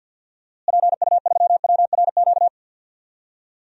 Morse code, heard as a keyed tone, about 51 wpm, 700 Hz. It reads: PU3WRQ
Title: DXCC-100 Call Signs - 6 Characters - Repeated 2x - 50wpm